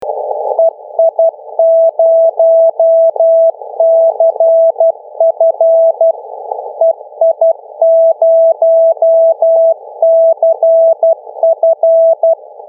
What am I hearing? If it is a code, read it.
Morse code, heard as a keyed tone, about 12 wpm, 660 Hz. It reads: EI0CF EI0CF
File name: mw / 0100 501 ei0cf finbar beacon